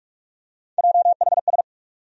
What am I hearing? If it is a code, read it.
Morse code, heard as a keyed tone, about 46 wpm, 700 Hz. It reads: JHS
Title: USA Call Sign Postfixes - 3 Letters 45wpm